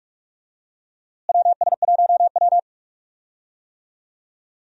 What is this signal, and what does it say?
Morse code, heard as a keyed tone, about 46 wpm, 700 Hz. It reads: WS1W
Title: USA Call Signs - 4 Characters - Repeated 2x - 45wpm